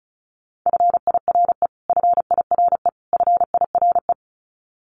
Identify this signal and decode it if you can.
Morse code, heard as a keyed tone, about 34 wpm, 700 Hz. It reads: FIRE FIRE FIRE
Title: Top 500 Words Repeated 35wpm